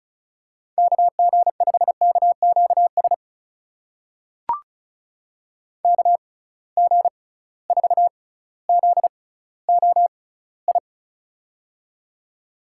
Morse code, heard as a keyed tone, about 36 wpm, spaced as for 18 wpm, 700 Hz. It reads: KG5KQS KG4ZOI
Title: USA Call Signs - 6 Characters - 6x Character Spacing - 35wpm